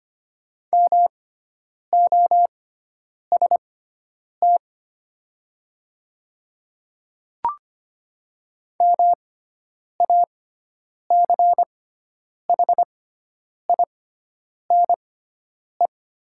Morse code, heard as a keyed tone, about 25 wpm, spaced as for 9 wpm, 700 Hz. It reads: MOST MACHINE
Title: 6x Character Spacing - Top 500 Words - 25wpm